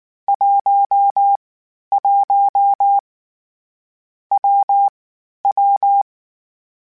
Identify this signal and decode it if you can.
Morse code, heard as a keyed tone, about 19 wpm, 800 Hz. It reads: 1 1 W W
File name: Difficult Combos / Difficult Combos-mp3s / WPJ WPJ1 @18